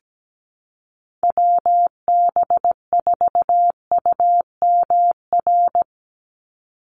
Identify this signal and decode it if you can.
Morse code, heard as a keyed tone, about 17 wpm, 700 Hz. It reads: WB4UMR
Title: USA Call Signs 17wpm